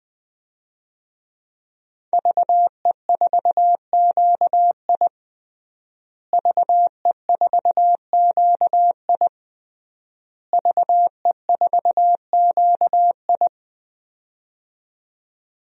Morse code, heard as a keyed tone, about 21 wpm, 700 Hz. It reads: VE4QI VE4QI VE4QI
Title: DXCC-100 Call Signs - 5 Characters - Repeated 3x - 20wpm